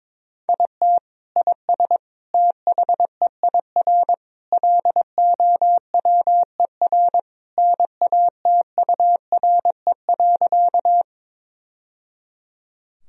Morse code, heard as a keyed tone, about 22 wpm, 700 Hz. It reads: IT IS THEIR LOWER NATURE.
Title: Sentences from Top 1000 Words - Review Part 4 - 22wpm